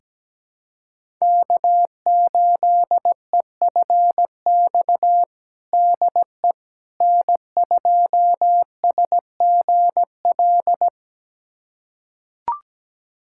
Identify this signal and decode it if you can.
Morse code, heard as a keyed tone, about 17 wpm, 700 Hz. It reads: K8EFX DE N2SGL
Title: USA Callsign DE Callsign - 5 Characters - 17wpm